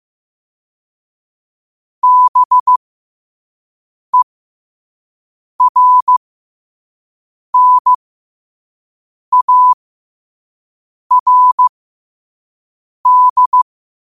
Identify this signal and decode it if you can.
Morse code, heard as a keyed tone, about 15 wpm, spaced as for 5 wpm, 1000 Hz. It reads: BERNARD
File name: qotdfiles / 5wpm